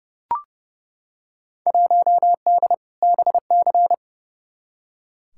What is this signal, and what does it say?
Morse code, heard as a keyed tone, about 30 wpm, 700 Hz. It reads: 1D BC
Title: Field Day Exchanges - 30wpm